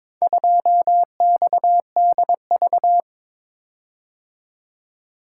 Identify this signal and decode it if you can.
Morse code, heard as a keyed tone, about 21 wpm, 700 Hz. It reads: 2XDV
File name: practice / 22wpm / Four Letter-Number Combinations - 22wpm